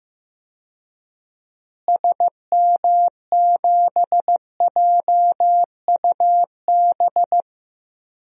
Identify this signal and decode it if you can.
Morse code, heard as a keyed tone, about 16 wpm, 700 Hz. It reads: SM7JUB
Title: DXCC-100 Call Signs - 6 Characters - Repeated 2x - 15wpm